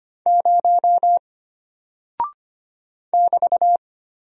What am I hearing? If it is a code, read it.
Morse code, heard as a keyed tone, about 25 wpm, 700 Hz. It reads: 0 <BT>
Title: Single Character (Letter:Number:Punctuation:Prosign) Alt 25wpm